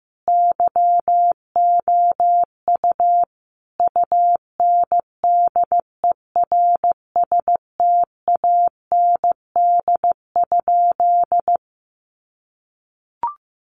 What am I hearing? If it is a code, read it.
Morse code, heard as a keyed tone, about 15 wpm, 700 Hz. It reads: YOU UNDERSTAND?
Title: Sentences from Top 1500 Words 15wpm